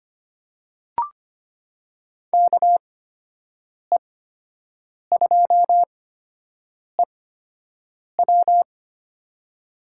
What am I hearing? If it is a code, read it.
Morse code, heard as a keyed tone, about 25 wpm, spaced as for 7 wpm, 700 Hz. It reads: KE2EW
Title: USA Call Signs - 5 Characters - 8x Character Spacing - 25wpm